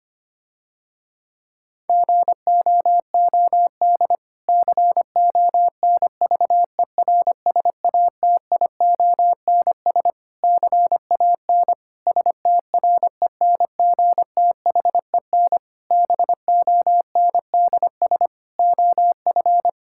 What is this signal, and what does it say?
Morse code, heard as a keyed tone, about 25 wpm, 700 Hz. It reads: GOOD CONVERSATIONS CAN STRENGTHEN BONDS OF
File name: practice / 25wpm / Difficult Sentences with Hints - 25wpm